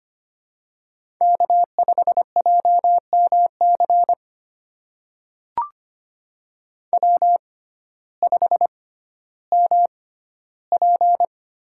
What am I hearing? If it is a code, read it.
Morse code, heard as a keyed tone, about 25 wpm, spaced as for 14 wpm, 700 Hz. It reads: K5JMC W5MP
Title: USA Call Signs - 6x Character Spacing - 25wpm